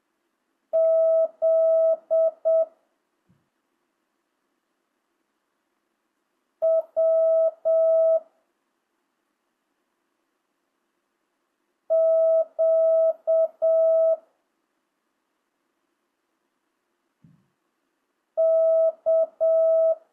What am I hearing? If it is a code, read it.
Morse code, heard as a keyed tone, about 7 wpm, 660 Hz. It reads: Z W Q K
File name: jh8cbh / 2025cw